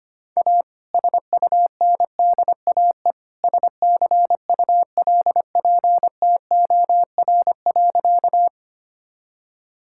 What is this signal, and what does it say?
Morse code, heard as a keyed tone, about 25 wpm, 700 Hz. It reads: A SUNDAE SCULPTOR.
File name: practice / 25wpm / All-Age Appropriate Jokes - 25wpm